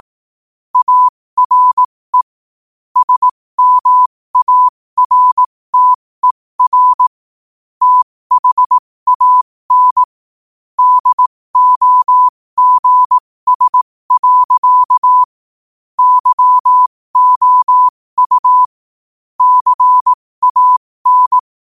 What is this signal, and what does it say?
Morse code, heard as a keyed tone, about 18 wpm, spaced as for 15 wpm, 1000 Hz. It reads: ARE SMARTER THAN DOGS. YOU CAN
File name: qotdfiles / 15wpm